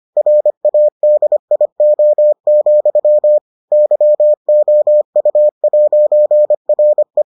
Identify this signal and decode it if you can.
Morse code, heard as a keyed tone, about 25 wpm, 600 Hz. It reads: RADIO, YOU'RE